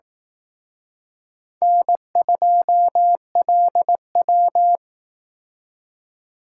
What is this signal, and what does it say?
Morse code, heard as a keyed tone, about 19 wpm, 700 Hz. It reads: N2LW
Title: Speed Racing - USA 1x2 Call Signs 15wpm